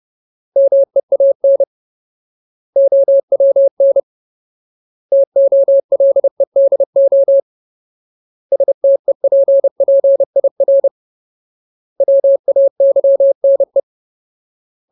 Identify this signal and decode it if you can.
Morse code, heard as a keyed tone, about 30 wpm, 550 Hz. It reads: MEAN OWN TOLEDO STEPPIR WAYNE
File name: Ultimate Word List / 30WPM Ultimate